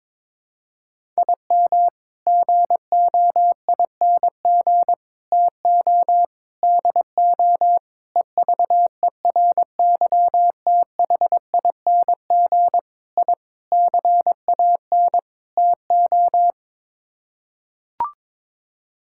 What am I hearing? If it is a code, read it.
Morse code, heard as a keyed tone, about 22 wpm, 700 Hz. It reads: IM GOING TO DO EVERYTHING I CAN TO
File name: practice / 22wpm / Sets of 8 Words - 22wpm